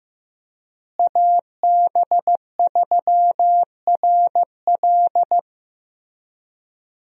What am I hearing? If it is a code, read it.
Morse code, heard as a keyed tone, about 15 wpm, 700 Hz. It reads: AB3RL
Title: USA Call Signs - 5 Characters - Repeated 3x - 15wpm